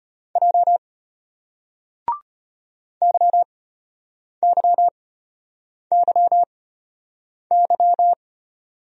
Morse code, heard as a keyed tone, about 34 wpm, spaced as for 8 wpm, 700 Hz. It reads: J YYYY